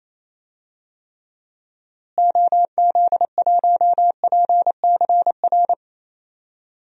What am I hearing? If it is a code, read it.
Morse code, heard as a keyed tone, about 28 wpm, 700 Hz. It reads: OZ1PCR
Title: DXCC-100 Call Signs - 6 Characters - Repeated 2x - 28wpm